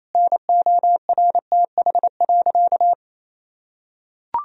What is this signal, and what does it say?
Morse code, heard as a keyed tone, about 28 wpm, 700 Hz. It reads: NORTH.
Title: Sentences from Top 1000 Words - Review Part 1 - 28wpm